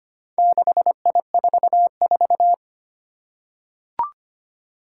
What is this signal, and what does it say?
Morse code, heard as a keyed tone, about 25 wpm, 700 Hz. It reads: 6I44
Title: Four Challenging Letter-Number Combinations 25wpm